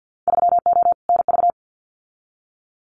Morse code, heard as a keyed tone, about 50 wpm, 700 Hz. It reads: 3O NV
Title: Winter Field Day Exchanges Encore 50wpm